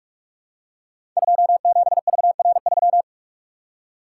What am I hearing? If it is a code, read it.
Morse code, heard as a keyed tone, about 44 wpm, 700 Hz. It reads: 27VR3
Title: Code Groups - 5 Letter:Numbers - 45wpm